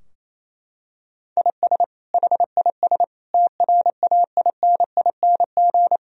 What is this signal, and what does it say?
Morse code, heard as a keyed tone, about 28 wpm, 700 Hz. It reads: IS HIS TRAINING
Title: Sentences from Top 1000 Words - Review Part 4 - 28wpm